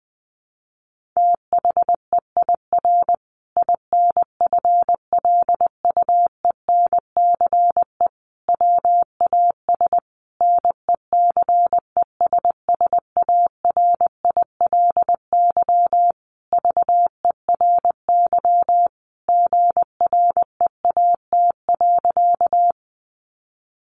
Morse code, heard as a keyed tone, about 20 wpm, 700 Hz. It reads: THEIR INFLUENCE WAS NECESSARILY VERY GREAT.
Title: Sentences from Top 1400 Words 20wpm